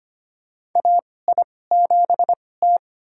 Morse code, heard as a keyed tone, about 25 wpm, spaced as for 18 wpm, 700 Hz. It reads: AI7T
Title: USA Call Signs - 2x Character Spacing - 25wpm